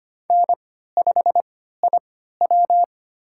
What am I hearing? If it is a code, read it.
Morse code, heard as a keyed tone, about 25 wpm, spaced as for 14 wpm, 700 Hz. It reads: N5IW
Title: USA Call Signs - 4 Characters - 3x Character Spacing - 25wpm